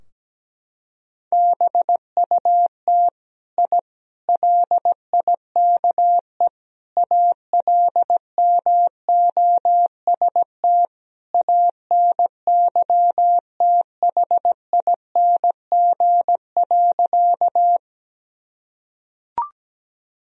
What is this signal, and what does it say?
Morse code, heard as a keyed tone, about 17 wpm, 700 Hz. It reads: BUT I LIKE ALMOST ANYTHING.
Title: Sentences from Top 1000 Words - Review Part 4 - 17wpm